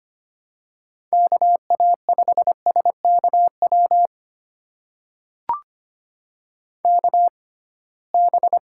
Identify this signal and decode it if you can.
Morse code, heard as a keyed tone, about 25 wpm, spaced as for 19 wpm, 700 Hz. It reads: KA5SKW KB
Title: USA Call Signs - 6 Characters - 6x Character Spacing - 25wpm